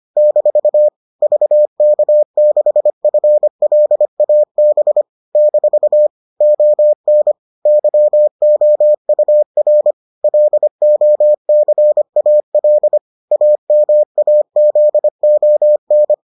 Morse code, heard as a keyed tone, about 25 wpm, 600 Hz. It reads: - VK6FLAB - ON YOUR LOCAL AMAZON